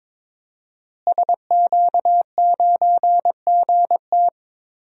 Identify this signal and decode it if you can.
Morse code, heard as a keyed tone, about 22 wpm, 700 Hz. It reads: SQ9GT
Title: DXCC-100 Call Signs - 5 Characters - Repeated 3x - 22wpm